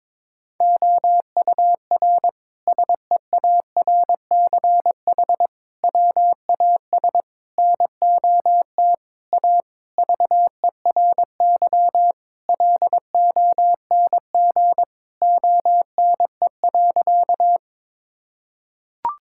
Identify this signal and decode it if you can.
Morse code, heard as a keyed tone, about 21 wpm, 700 Hz. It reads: OUR SEARCH WAS NOT A VERY LONG ONE.
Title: Sentences from Top 1000 Words - Review Part 5 - 22wpm